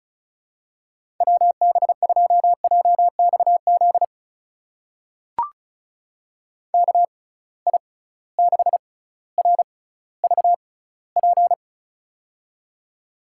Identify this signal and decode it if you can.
Morse code, heard as a keyed tone, about 36 wpm, spaced as for 18 wpm, 700 Hz. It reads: WB2JXZ KI6RVP
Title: USA Call Signs - 6 Characters - 6x Character Spacing - 35wpm